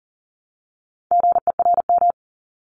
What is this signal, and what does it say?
Morse code, heard as a keyed tone, about 40 wpm, 700 Hz. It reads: GERM
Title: Multi-letter Phonemes Repeated 3x 40wpm